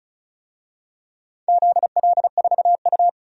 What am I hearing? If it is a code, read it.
Morse code, heard as a keyed tone, about 36 wpm, 700 Hz. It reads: ZL4U
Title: DXCC-100 Call Signs - 4 Characters - 35wpm